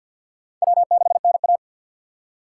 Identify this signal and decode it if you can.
Morse code, heard as a keyed tone, about 53 wpm, 700 Hz. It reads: W6NA
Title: USA Call Signs - 4 Characters - 2x Character Spacing - 50wpm